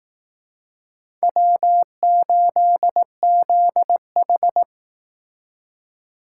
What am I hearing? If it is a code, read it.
Morse code, heard as a keyed tone, about 19 wpm, 700 Hz. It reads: W8ZH